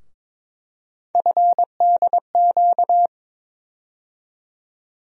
Morse code, heard as a keyed tone, about 23 wpm, 700 Hz. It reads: FDQ
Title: USA Call Sign Postfixes - 3 Letters 22wpm